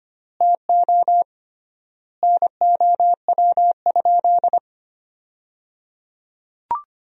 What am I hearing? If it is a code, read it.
Morse code, heard as a keyed tone, about 25 wpm, 700 Hz. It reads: TO NOW?
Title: 3x Word-Spacing - Sentences from Top 100 Words - 25wpm